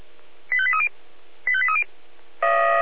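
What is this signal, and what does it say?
Five-tone selective call (ZVEI), heard as quick runs of five tones, tones of about 70 ms. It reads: 85830 85830
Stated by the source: ZVEI five-tone call 85278 received by SDR.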